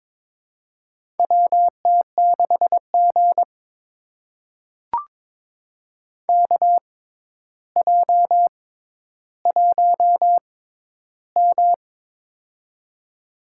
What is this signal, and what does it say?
Morse code, heard as a keyed tone, about 21 wpm, spaced as for 11 wpm, 700 Hz. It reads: WT6G KJ1M
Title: USA Call Signs - 4 Characters - 6x Character Spacing - 22wpm